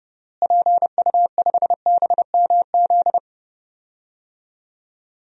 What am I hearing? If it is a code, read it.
Morse code, heard as a keyed tone, about 30 wpm, 700 Hz. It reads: PU5BMZ
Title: DXCC-100 Call Signs - 6 Characters - Repeated 3x - 30wpm